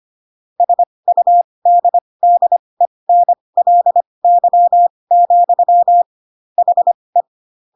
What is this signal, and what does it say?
Morse code, heard as a keyed tone, about 25 wpm, spaced as for 20 wpm, 700 Hz. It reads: SUDDENLY, HE